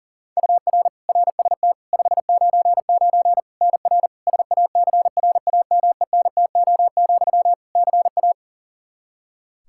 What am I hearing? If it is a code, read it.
Morse code, heard as a keyed tone, about 37 wpm, 700 Hz. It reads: UR RST 599 NR SACRAMENTO, CA
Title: Practice QSO - Exchange 1 - Beginning - 40wpm